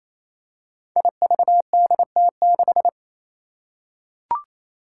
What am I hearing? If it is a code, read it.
Morse code, heard as a keyed tone, about 27 wpm, 700 Hz. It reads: IVDT6
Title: Five Challenging Letter-Number Combinations 28wpm